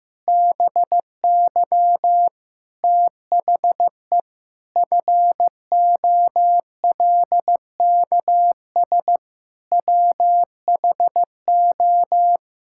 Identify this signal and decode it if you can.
Morse code, heard as a keyed tone, about 15 wpm, 700 Hz. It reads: BY THE FOLKS WHO
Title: Pangram Challenge v2 - 15wpm